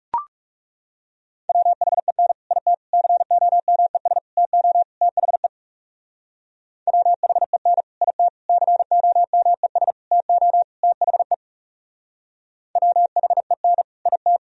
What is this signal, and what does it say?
Morse code, heard as a keyed tone, about 38 wpm, 700 Hz. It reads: WHEN IT COMES TO THE WHEN IT COMES TO THE WHEN IT
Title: Speed-Racing - Sets of 5 Words 30wpm